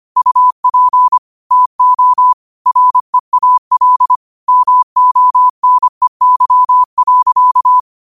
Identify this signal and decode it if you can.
Morse code, heard as a keyed tone, about 25 wpm, 1000 Hz. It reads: UP TO REAL MONEY.